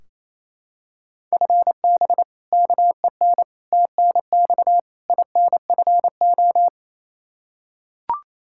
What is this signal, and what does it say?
Morse code, heard as a keyed tone, about 28 wpm, 700 Hz. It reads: FB KEN TNX INFO
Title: Practice QSO - Exchange 3 - More Info - 28wpm